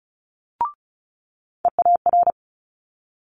Morse code, heard as a keyed tone, about 36 wpm, 700 Hz. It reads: EAR
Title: Multi-letter Phonemes Quick Fire 35wpm